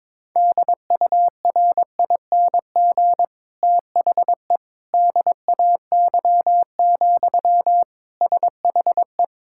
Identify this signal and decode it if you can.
Morse code, heard as a keyed tone, about 22 wpm, 700 Hz. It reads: DURING THE DAY, SHE
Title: Sentences from Top 1000 Words - Review Part 6 - 22wpm